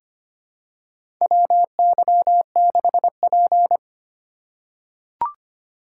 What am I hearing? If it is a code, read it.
Morse code, heard as a keyed tone, about 25 wpm, 700 Hz. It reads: WY6P
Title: USA Call Signs - 4 Characters - 4x Character Spacing - 25wpm